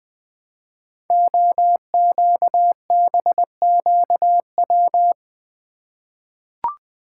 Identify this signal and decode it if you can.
Morse code, heard as a keyed tone, about 20 wpm, 700 Hz. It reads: OQBQW
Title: Code Groups - 5 Letters - 20wpm